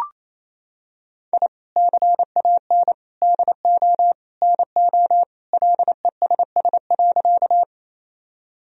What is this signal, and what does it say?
Morse code, heard as a keyed tone, about 28 wpm, 700 Hz. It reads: I CAN DO NO LESS.
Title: Sentences from Top 1000 Words - Review Part 2 - 28wpm